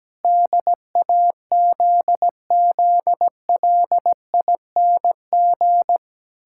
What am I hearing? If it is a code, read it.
Morse code, heard as a keyed tone, about 17 wpm, 700 Hz. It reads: DAZZLING
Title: Pangram Challenge v2 - 17wpm